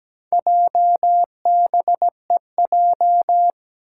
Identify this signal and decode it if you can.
Morse code, heard as a keyed tone, about 17 wpm, 700 Hz. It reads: JBEJ
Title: Four Challenging Letter-Number Combinations 17wpm